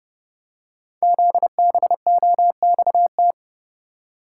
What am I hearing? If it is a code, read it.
Morse code, heard as a keyed tone, about 30 wpm, 700 Hz. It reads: ZBOXT
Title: Code Groups - 5 Letters - 30wpm